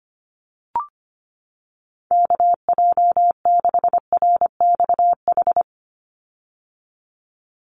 Morse code, heard as a keyed tone, about 25 wpm, 700 Hz. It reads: KJ6RXH
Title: USA Call Signs 25wpm